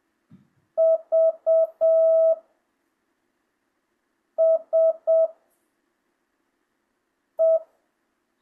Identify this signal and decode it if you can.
Morse code, heard as a keyed tone, about 7 wpm, 660 Hz. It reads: V S E
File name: jh8cbh / 2025cw